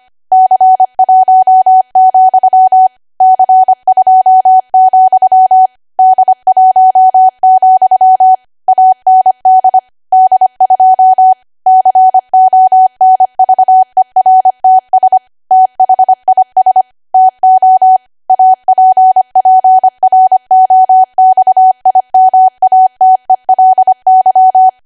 Morse code, heard as a keyed tone, about 25 wpm, 750 Hz. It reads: C1, C2, D1, AND D2 CONVERTS THIS TO APPROXIMATELY
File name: Archive / 25 WPM / 25WPM